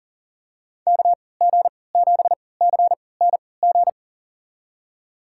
Morse code, heard as a keyed tone, about 39 wpm, spaced as for 22 wpm, 700 Hz. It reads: KG7CNG
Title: USA Call Signs - 6 Characters - 3x Character Spacing - 40wpm